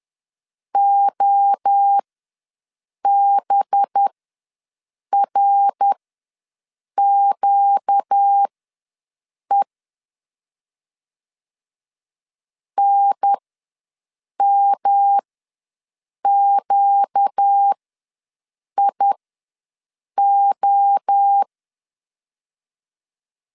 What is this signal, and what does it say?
Morse code, heard as a keyed tone, about 10 wpm, spaced as for 6 wpm, 800 Hz. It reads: OBRQE NMQIO
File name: LEZIONI / 5 LBVC / 5 LB ws 10wpm 3616